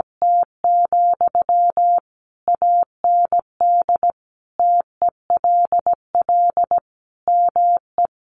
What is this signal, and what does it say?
Morse code, heard as a keyed tone, about 17 wpm, 700 Hz. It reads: T, AND TELL ME
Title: Sentences from Top 200 Words 17wpm